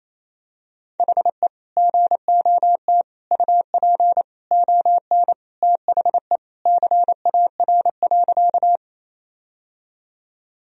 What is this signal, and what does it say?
Morse code, heard as a keyed tone, about 28 wpm, 700 Hz. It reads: HE GOT UP ON THE CAR.
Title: Sentences from Top 1000 Words - Review Part 1 - 28wpm